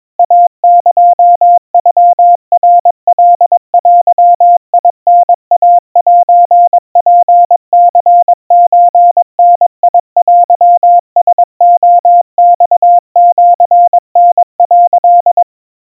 Morse code (Japanese Wabun code), heard as a keyed tone, about 22 wpm, 700 Hz. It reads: イエノナカデタイセツニソダテラレマシタ」
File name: 80 cpm 700 Hz Crispy